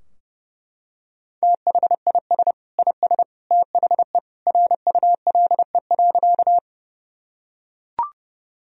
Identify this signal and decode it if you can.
Morse code, heard as a keyed tone, about 31 wpm, 700 Hz. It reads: THIS IS THE RULE.
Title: Sentences from Top 1000 Words - Review Part 2 - 30wpm